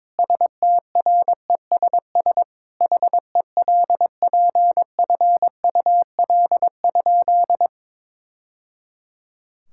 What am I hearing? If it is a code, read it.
Morse code, heard as a keyed tone, about 22 wpm, 700 Hz. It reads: STRESS HELPFUL?
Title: Sentences from Top 1000 Words - Review Part 6 - 22wpm